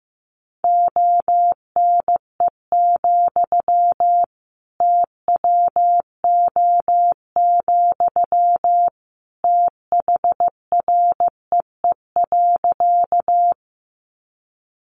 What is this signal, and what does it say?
Morse code, heard as a keyed tone, about 15 wpm, 700 Hz. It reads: ONE, TWO, THREE.
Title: Sentences from Top 200 Words 15wpm